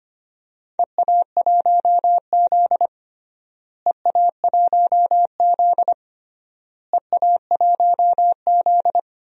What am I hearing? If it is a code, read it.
Morse code, heard as a keyed tone, about 25 wpm, 700 Hz. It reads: EA1Z EA1Z EA1Z
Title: DXCC-100 Call Signs - 4 Characters - Repeated 3x - 25wpm